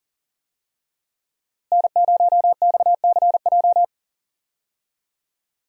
Morse code, heard as a keyed tone, about 40 wpm, 700 Hz. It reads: N0XCJ
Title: USA Call Signs - 5 Characters - Repeated 3x - 40wpm